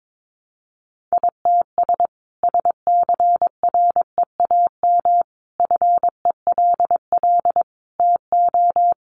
Morse code, heard as a keyed tone, about 22 wpm, 700 Hz. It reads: ITS SCREAM FELL TO
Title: A Sound Of Thunder 22wpm